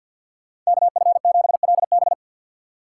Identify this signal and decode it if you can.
Morse code, heard as a keyed tone, about 63 wpm, 690 Hz. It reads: KF7LB